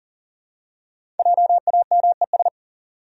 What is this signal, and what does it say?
Morse code, heard as a keyed tone, about 40 wpm, 700 Hz. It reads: JAMES
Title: USA Common Names - Max 5 Letters - 40wpm